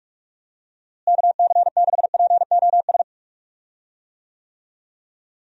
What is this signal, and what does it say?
Morse code, heard as a keyed tone, about 47 wpm, 700 Hz. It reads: KC6POS
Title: USA Call Signs - 6 Characters - Repeated 2x - 45wpm